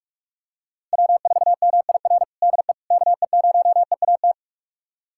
Morse code, heard as a keyed tone, about 46 wpm, 700 Hz. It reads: W4MIR DE KE0EAT
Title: USA Callsign DE Callsign 45wpm